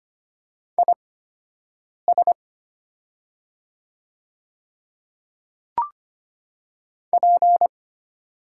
Morse code, heard as a keyed tone, about 25 wpm, spaced as for 7 wpm, 700 Hz. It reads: IS P
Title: 8x Character Spacing - Top 300 Words - 25wpm